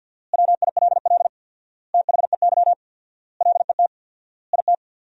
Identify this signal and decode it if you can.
Morse code, heard as a keyed tone, about 50 wpm, 700 Hz. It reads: WILL THEY LET IT
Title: 4x Word-Spacing - Sentences from Top 400 Words - 50wpm